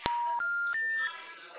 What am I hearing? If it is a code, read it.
Special information tone, with noise.